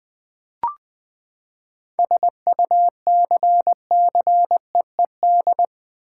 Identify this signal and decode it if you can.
Morse code, heard as a keyed tone, about 21 wpm, 700 Hz. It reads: SUCCEED
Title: Seven Letter Words - 20wpm